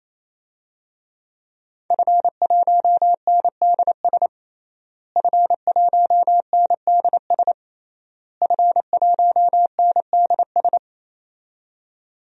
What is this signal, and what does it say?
Morse code, heard as a keyed tone, about 28 wpm, 700 Hz. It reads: F1NDS F1NDS F1NDS
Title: DXCC-100 Call Signs - 5 Characters - Repeated 3x - 28wpm